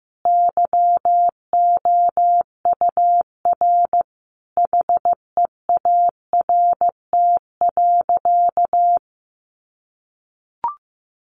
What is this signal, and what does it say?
Morse code, heard as a keyed tone, about 15 wpm, 700 Hz. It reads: YOUR HEART.
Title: Sentences from Top 1400 Words 15wpm